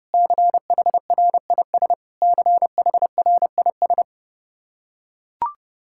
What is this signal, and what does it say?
Morse code, heard as a keyed tone, about 30 wpm, 700 Hz. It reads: CHRIS CHRIS
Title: Practice QSO - Exchange 1 - Beginning - 30wpm